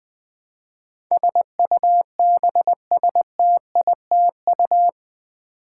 Morse code, heard as a keyed tone, about 20 wpm, 700 Hz. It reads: SUBSTITU
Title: Build A Word - 4 Letters At A Time - 12-Letter Words - 20wpm